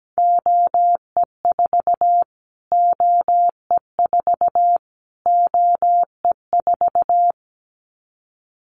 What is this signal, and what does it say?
Morse code, heard as a keyed tone, about 17 wpm, 700 Hz. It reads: OE4 OE4 OE4
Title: Callsign Prefixes for Top 25 DXCC - Repeated 3x - 17wpm